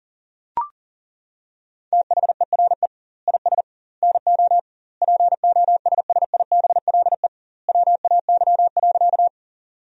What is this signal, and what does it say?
Morse code, heard as a keyed tone, about 40 wpm, 700 Hz. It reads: THERE IS NO POSSIBLE WAY.
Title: 2x Word-Spacing - Sentences from Top 500 Words - 40wpm